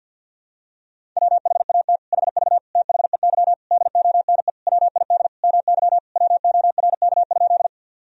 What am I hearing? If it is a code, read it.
Morse code, heard as a keyed tone, about 46 wpm, 700 Hz. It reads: WHAT HV THEY DONE WID MY WORK?
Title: Sentences from Top 500 Words and Abbreviations 50wpm